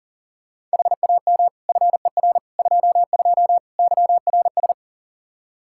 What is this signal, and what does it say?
Morse code, heard as a keyed tone, about 40 wpm, 700 Hz. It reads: HAM FER 22 YRS
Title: Practice QSO - Exchange 3 - More Info - 40wpm